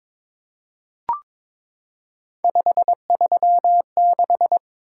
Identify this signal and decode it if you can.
Morse code, heard as a keyed tone, about 22 wpm, 700 Hz. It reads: 536
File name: practice / 22wpm / Three Letter-Number Combinations - 22wpm